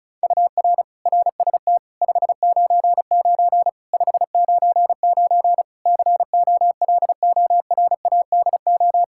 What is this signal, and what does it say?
Morse code, heard as a keyed tone, about 34 wpm, 700 Hz. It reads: UR RST 599 599 COLORADO
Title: K1USN SST QSO Practice 35wpm